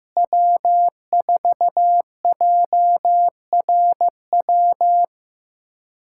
Morse code, heard as a keyed tone, about 15 wpm, 700 Hz. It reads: W4JRW
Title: USA Callsign DE Callsign - 5 Characters - 15wpm